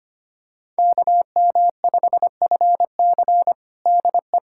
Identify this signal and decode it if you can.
Morse code, heard as a keyed tone, about 25 wpm, 700 Hz. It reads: KM5FC DE
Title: USA Callsign DE Callsign - 5 Characters - 25wpm